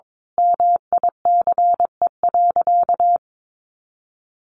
Morse code, heard as a keyed tone, about 22 wpm, 700 Hz. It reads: MICE.
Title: A Sound Of Thunder 22wpm